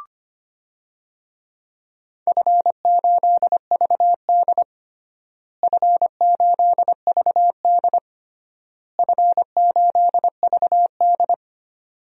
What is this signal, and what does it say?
Morse code, heard as a keyed tone, about 25 wpm, 700 Hz. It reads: F8VD F8VD F8VD
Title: DXCC-100 Call Signs - 4 Characters - Repeated 3x - 25wpm